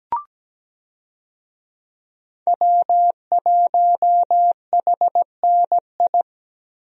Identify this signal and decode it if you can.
Morse code, heard as a keyed tone, about 17 wpm, 700 Hz. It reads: W1HNI
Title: USA Call Signs - 5 Characters - Repeated 2x - 17wpm